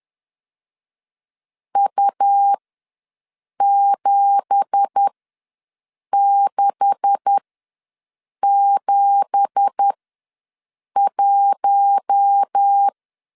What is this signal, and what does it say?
Morse code, heard as a keyed tone, about 10 wpm, spaced as for 6 wpm, 800 Hz. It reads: U7671
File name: LEZIONI / 9 2867 ws 10wpm 4215